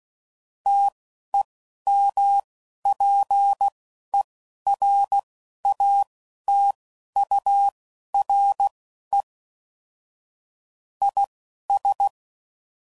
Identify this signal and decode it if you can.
Morse code, heard as a keyed tone, about 16 wpm, spaced as for 11 wpm, 800 Hz. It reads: TEMPERATURE IS
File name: CW Code Groups @10WPM / Weather Sets / Weather 5 5